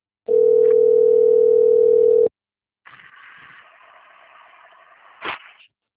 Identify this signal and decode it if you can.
Ringback tone (North America), 440 + 480 Hz.
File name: rg-1600-unknown- - - .3062